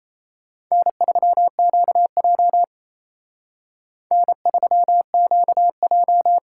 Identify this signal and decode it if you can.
Morse code, heard as a keyed tone, about 35 wpm, 700 Hz. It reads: N3QJ N3QJ